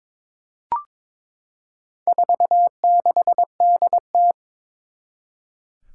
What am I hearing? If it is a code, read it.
Morse code, heard as a keyed tone, about 22 wpm, 700 Hz. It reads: 46DT
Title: Four Challenging Letter-Number Combinations 22wpm